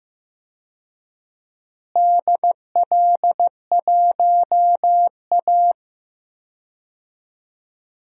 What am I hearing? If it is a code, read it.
Morse code, heard as a keyed tone, about 15 wpm, 700 Hz. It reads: DL1A
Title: DXCC-100 Call Signs - 4 Characters - Repeated 3x - 15wpm